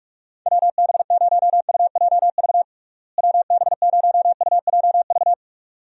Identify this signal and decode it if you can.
Morse code, heard as a keyed tone, about 46 wpm, 700 Hz. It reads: WB0UJV WB0UJV
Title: USA Call Signs - 6 Characters - Repeated 2x - 45wpm